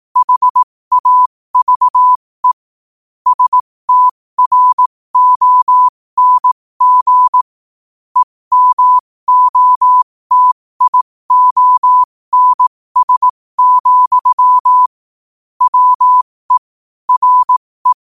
Morse code, heard as a keyed tone, about 18 wpm, spaced as for 15 wpm, 1000 Hz. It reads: HAVE STRONG EMOTIONS, WE RE
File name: qotdfiles / 15wpm